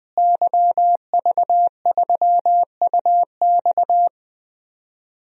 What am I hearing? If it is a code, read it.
Morse code, heard as a keyed tone, about 20 wpm, 700 Hz. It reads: YV3UX
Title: DXCC-100 Call Signs - 5 Characters - Repeated 3x - 20wpm